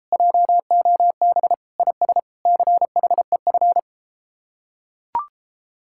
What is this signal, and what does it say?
Morse code, heard as a keyed tone, about 31 wpm, 700 Hz. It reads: JOB IS CHEF